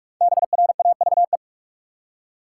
Morse code, heard as a keyed tone, about 46 wpm, 700 Hz. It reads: BRAVE
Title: Five Letter Words - 45wpm